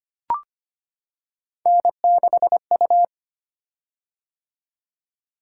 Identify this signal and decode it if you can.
Morse code, heard as a keyed tone, about 25 wpm, 700 Hz. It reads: N6U
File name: Three Letter-Number Combinations - 25wpm